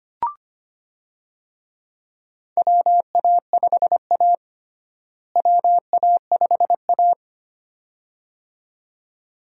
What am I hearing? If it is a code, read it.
Morse code, heard as a keyed tone, about 25 wpm, 700 Hz. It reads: WA5A WA5A
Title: USA Call Signs - 4 Characters - Repeated 2x - 25wpm